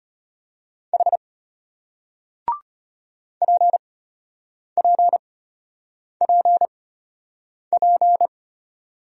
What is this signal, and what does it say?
Morse code, heard as a keyed tone, about 34 wpm, spaced as for 8 wpm, 700 Hz. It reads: H PPPP